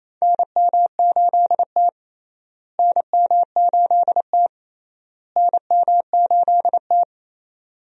Morse code, heard as a keyed tone, about 28 wpm, 700 Hz. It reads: NM8T NM8T NM8T
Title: USA Call Signs - 4 Characters - Repeated 3x - 28wpm